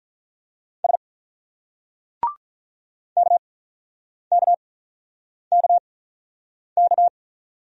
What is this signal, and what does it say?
Morse code, heard as a keyed tone, about 47 wpm, spaced as for 9 wpm, 700 Hz. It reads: S KKKK